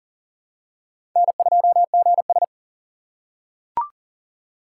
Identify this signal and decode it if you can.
Morse code, heard as a keyed tone, about 40 wpm, 700 Hz. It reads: N2GS
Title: USA Call Signs - 4 Characters - 4x Character Spacing - 40wpm